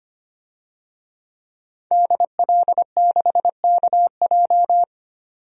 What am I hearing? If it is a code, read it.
Morse code, heard as a keyed tone, about 25 wpm, 700 Hz. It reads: DL6KJ
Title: DXCC-100 Call Signs - 5 Characters - 25wpm